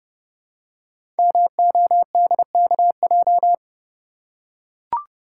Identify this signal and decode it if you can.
Morse code, heard as a keyed tone, about 30 wpm, 700 Hz. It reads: MODKJ
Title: Code Groups - 5 Letter:Numbers - 30wpm